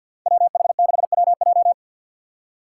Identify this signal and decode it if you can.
Morse code, heard as a keyed tone, about 52 wpm, 700 Hz. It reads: WH6WJ
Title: USA Call Signs - 3x Character Spacing - 50wpm